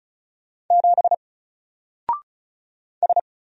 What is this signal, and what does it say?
Morse code, heard as a keyed tone, about 36 wpm, 700 Hz. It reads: 7 S
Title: Single Letter-Number Alt 35wpm